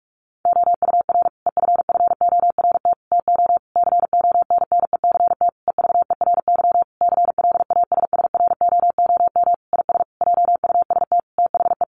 Morse code, heard as a keyed tone, about 44 wpm, 700 Hz. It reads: OUR EFFORT TO CONNECT EVERY CLASSROOM IS JUST THE
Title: Sentences from Top 1500 Words 45wpm